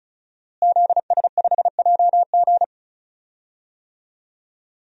Morse code, heard as a keyed tone, about 36 wpm, 700 Hz. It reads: ZS5JG
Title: DXCC-100 Call Signs - 5 Characters - Repeated 3x - 35wpm